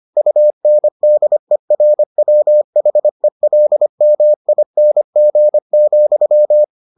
Morse code, heard as a keyed tone, about 25 wpm, 600 Hz. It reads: UNDERWHELMING,